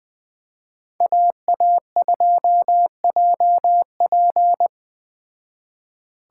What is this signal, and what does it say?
Morse code, heard as a keyed tone, about 20 wpm, 700 Hz. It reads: AA2JP
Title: Speed Racing - USA 2x2 Call Signs 17wpm